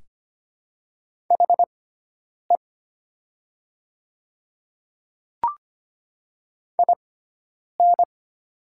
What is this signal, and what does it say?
Morse code, heard as a keyed tone, about 25 wpm, spaced as for 9 wpm, 700 Hz. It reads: HE IN
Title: 6x Character Spacing - Top 100 Words - 25wpm